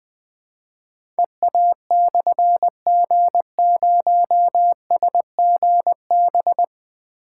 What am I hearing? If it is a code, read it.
Morse code, heard as a keyed tone, about 20 wpm, 700 Hz. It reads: EA/G0SGB
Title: Difficult Call Signs 20wpm